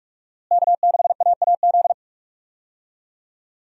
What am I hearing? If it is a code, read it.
Morse code, heard as a keyed tone, about 46 wpm, 700 Hz. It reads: K6AAZ
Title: USA Call Signs - 5 Characters - Repeated 3x - 45wpm